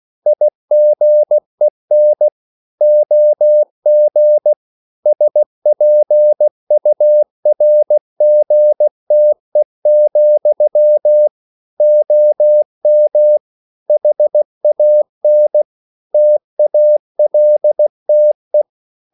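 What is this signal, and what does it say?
Morse code, heard as a keyed tone, about 16 wpm, 600 Hz. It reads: IGEN OG SPURGTE, OM HAN TALTE
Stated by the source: Slow (QRS) Morse code at 16 wpm, from CW med Gnister.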